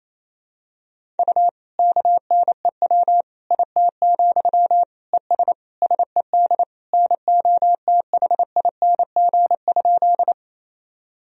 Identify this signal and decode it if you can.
Morse code, heard as a keyed tone, about 28 wpm, 700 Hz. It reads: U KNEW IT, ES SED NOTHING?
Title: Sentences from Top 500 Words and Abbreviations 28wpm